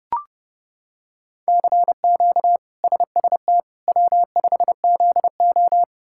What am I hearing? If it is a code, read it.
Morse code, heard as a keyed tone, about 30 wpm, 700 Hz. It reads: CQ SST W5ZO
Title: K1USN SST QSO Practice - 30wpm